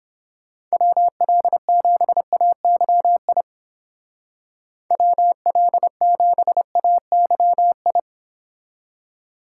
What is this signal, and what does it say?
Morse code, heard as a keyed tone, about 31 wpm, 700 Hz. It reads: WL7AYI WL7AYI